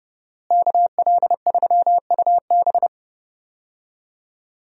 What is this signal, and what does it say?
Morse code, heard as a keyed tone, about 30 wpm, 700 Hz. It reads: KL3UB
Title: USA Call Signs - 5 Characters - Repeated 2x - 30wpm